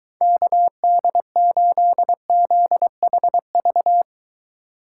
Morse code, heard as a keyed tone, about 23 wpm, 700 Hz. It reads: KD8ZHV